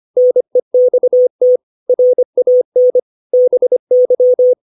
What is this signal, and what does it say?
Morse code, heard as a keyed tone, about 25 wpm, 500 Hz. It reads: NEXT RAN BY
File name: 632 words / 632 Words 25WPM